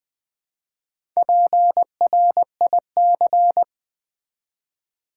Morse code, heard as a keyed tone, about 20 wpm, 700 Hz. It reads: PRIC
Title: Build A Word - 1 Letter At A Time - 3 to 5-Letter Words - 20wpm